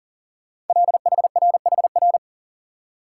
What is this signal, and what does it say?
Morse code, heard as a keyed tone, about 40 wpm, 700 Hz. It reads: LHRHR
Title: Code Groups - 5 Letter:Numbers - 40wpm